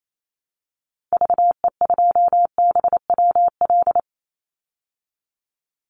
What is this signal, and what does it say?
Morse code, heard as a keyed tone, about 28 wpm, 700 Hz. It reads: VE2BWL
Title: Call Signs Repeated 2x - 28wpm